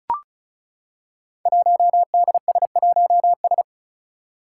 Morse code, heard as a keyed tone, about 34 wpm, 700 Hz. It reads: 1DS1S
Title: Five Challenging Letter-Number Combinations 35wpm